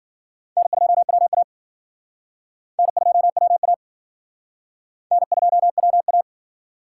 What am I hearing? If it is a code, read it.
Morse code, heard as a keyed tone, about 51 wpm, 700 Hz. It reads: N2WA N2WA N2WA